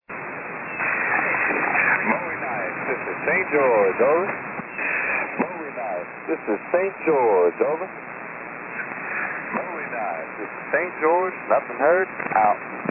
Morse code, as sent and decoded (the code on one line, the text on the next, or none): none